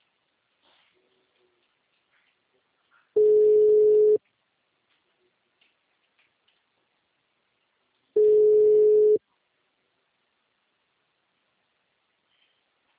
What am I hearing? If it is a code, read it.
Ringback tone (Europe, 425 Hz).